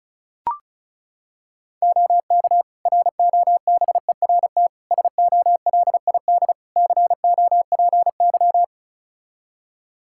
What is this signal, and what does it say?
Morse code, heard as a keyed tone, about 34 wpm, 700 Hz. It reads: OK ROBERT SOLID COPY
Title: K1USN SST QSO Practice 35wpm